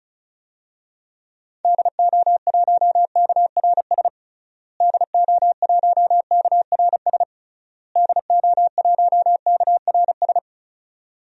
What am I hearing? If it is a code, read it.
Morse code, heard as a keyed tone, about 34 wpm, 700 Hz. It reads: DO1KRS DO1KRS DO1KRS
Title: DXCC-100 Call Signs - 6 Characters - Repeated 3x - 35wpm